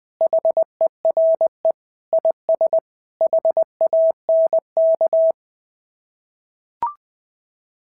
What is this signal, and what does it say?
Morse code, heard as a keyed tone, about 20 wpm, 650 Hz. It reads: HERE IS HANK
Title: SOTA QSO Practice 20wpm